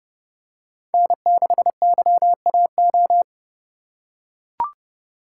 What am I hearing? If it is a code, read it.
Morse code, heard as a keyed tone, about 30 wpm, 700 Hz. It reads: N6YAO
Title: USA Call Signs - 5 Characters - 3x Character Spacing - 30wpm